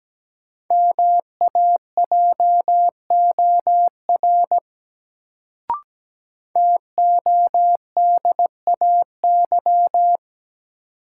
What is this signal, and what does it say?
Morse code, heard as a keyed tone, about 17 wpm, 700 Hz. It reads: MAJOR TODAY
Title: Sentences from Top 600 Words - New Word Drill - 17wpm